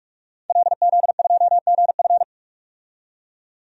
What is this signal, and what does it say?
Morse code, heard as a keyed tone, about 46 wpm, 700 Hz. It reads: LZ2GF
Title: DXCC-100 Call Signs - 5 Characters - 45wpm